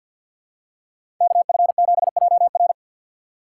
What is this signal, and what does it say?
Morse code, heard as a keyed tone, about 50 wpm, 700 Hz. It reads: KF7JR
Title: USA Call Signs - 5 Characters - 6x Character Spacing - 50wpm